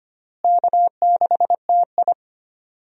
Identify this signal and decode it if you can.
Morse code, heard as a keyed tone, about 25 wpm, 700 Hz. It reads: K6TI
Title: USA Call Signs - 4 Characters - Repeated 2x - 25wpm